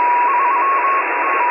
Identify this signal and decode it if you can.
Morse code, heard as a keyed tone, about 22 wpm, 1100 Hz, with radio noise. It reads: 3 T